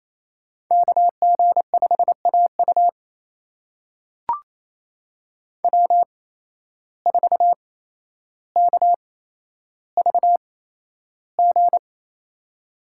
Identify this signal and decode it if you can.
Morse code, heard as a keyed tone, about 28 wpm, spaced as for 12 wpm, 700 Hz. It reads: KG5AU W4KVG
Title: USA Call Signs - 5 Characters - 8x Character Spacing - 28wpm